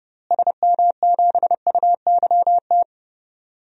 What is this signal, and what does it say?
Morse code, heard as a keyed tone, about 30 wpm, 700 Hz. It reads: SM7UYT
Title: DXCC-100 Call Signs - 6 Characters - Repeated 2x - 30wpm